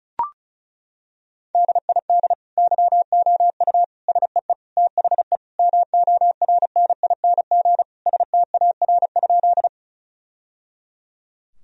Morse code, heard as a keyed tone, about 36 wpm, 700 Hz. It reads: DID YOU SEE THE MORNING STAR?
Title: Sentences from Top 1000 Words - Review Part 2 - 35wpm